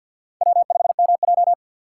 Morse code, heard as a keyed tone, about 53 wpm, 690 Hz. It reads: W5MJ
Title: USA Call Signs - 4 Characters - 6x Character Spacing - 50wpm